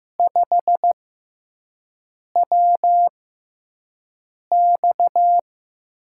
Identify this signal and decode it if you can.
Morse code, heard as a keyed tone, about 15 wpm, spaced as for 5 wpm, 700 Hz. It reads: 5WX
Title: USA Call Signs - 6x Character Spacing - 15wpm